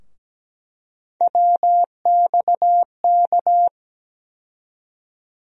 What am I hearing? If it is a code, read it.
Morse code, heard as a keyed tone, about 17 wpm, 700 Hz. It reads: WXK
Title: USA Call Sign Postfixes - 3 Letters 17wpm